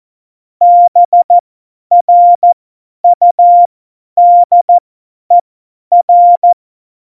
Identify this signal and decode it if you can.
Morse code, heard as a keyed tone, about 14 wpm, spaced as for 10 wpm, 700 Hz. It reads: BRUDER
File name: Morsekurs / Witze / Witz 15. August 17, Tempo 50 F70